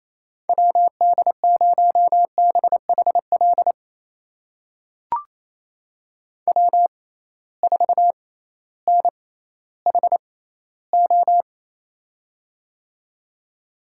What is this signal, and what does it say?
Morse code, heard as a keyed tone, about 28 wpm, spaced as for 15 wpm, 700 Hz. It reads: WD0BHL W4NHO
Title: USA Call Signs - 6x Character Spacing - 28wpm